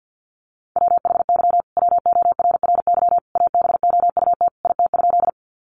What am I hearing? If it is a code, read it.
Morse code, heard as a keyed tone, about 50 wpm, 700 Hz. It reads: WHY WORRY ABOUT IT?
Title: Sentences from Top 1100 Words 50wpm